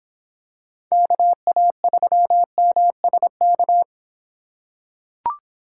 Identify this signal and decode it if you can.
Morse code, heard as a keyed tone, about 25 wpm, 700 Hz. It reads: KA3MSK